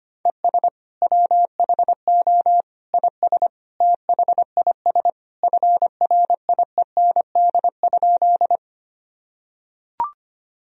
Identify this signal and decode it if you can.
Morse code, heard as a keyed tone, about 25 wpm, 700 Hz. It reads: ES WHO IS THIS FRIEND?
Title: Sentences from Top 300 Words and Abbreviations 25wpm